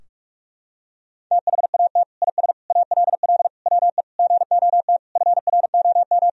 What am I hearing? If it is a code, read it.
Morse code, heard as a keyed tone, about 46 wpm, 700 Hz. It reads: THAT IS ALL WE GOT FROM
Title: Sentences from Top 1000 Words - Review Part 2 - 45wpm